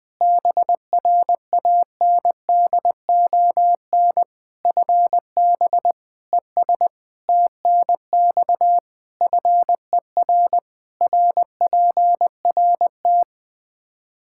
Morse code, heard as a keyed tone, about 20 wpm, 700 Hz. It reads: BRANDON FB ES TNX FER RPRT
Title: K1USN SST QSO Practice 20wpm